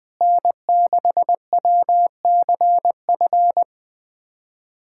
Morse code, heard as a keyed tone, about 20 wpm, 700 Hz. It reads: N6WCF
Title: USA Callsign DE Callsign - 5 Characters - 20wpm